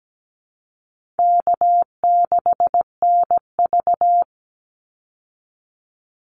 Morse code, heard as a keyed tone, about 17 wpm, 700 Hz. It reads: K6NV
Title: Easy Call Signs Repeated 3x - 17wpm